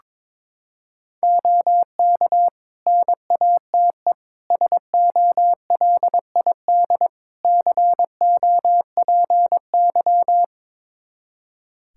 Morse code, heard as a keyed tone, about 22 wpm, 700 Hz. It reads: OK NATE SOLID COPY
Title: Practice QSO - Exchange 3 - More Info - 22wpm